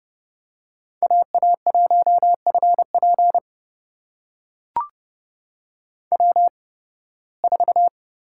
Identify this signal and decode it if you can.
Morse code, heard as a keyed tone, about 30 wpm, spaced as for 20 wpm, 700 Hz. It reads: AA1FP W4
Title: USA Call Signs - 5 Characters - 8x Character Spacing - 30wpm